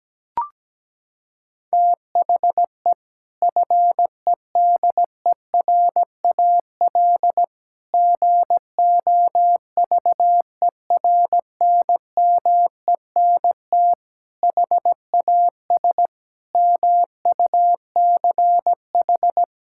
Morse code, heard as a keyed tone, about 17 wpm, 700 Hz. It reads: THE FEDERAL GOVERNMENT HAS MUCH
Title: Sentences from Top 1000 Words - Review Part 5 - 17wpm